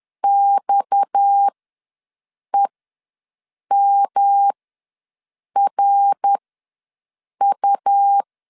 Morse code, heard as a keyed tone, about 10 wpm, spaced as for 6 wpm, 800 Hz. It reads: XEMRU
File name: LEZIONI / 6 PKX0 / 6 PKX ws 10wpm 3706